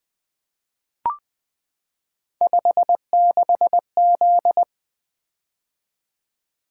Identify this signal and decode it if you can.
Morse code, heard as a keyed tone, about 20 wpm, 700 Hz. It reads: 56Z
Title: Three Letter-Number Combinations - 20wpm